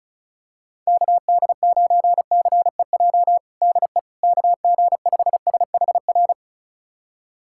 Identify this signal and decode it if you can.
Morse code, heard as a keyed tone, about 36 wpm, 690 Hz. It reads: KD9CEJ DE KG5SHR
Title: USA Callsign DE Callsign - 6 Characters - 35wpm